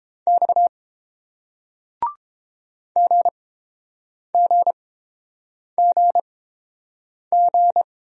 Morse code, heard as a keyed tone, about 30 wpm, spaced as for 7 wpm, 700 Hz. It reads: X GGGG